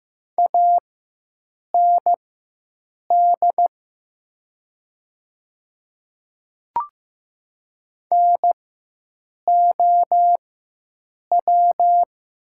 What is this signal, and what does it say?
Morse code, heard as a keyed tone, about 15 wpm, spaced as for 7 wpm, 700 Hz. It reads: AND NOW
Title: 4x Character Spacing - Top 100 Words - 15wpm